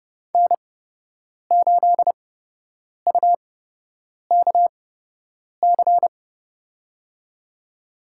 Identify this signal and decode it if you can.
Morse code, heard as a keyed tone, about 30 wpm, spaced as for 8 wpm, 700 Hz. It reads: N8UKC
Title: USA Call Signs - 8x Character Spacing - 30wpm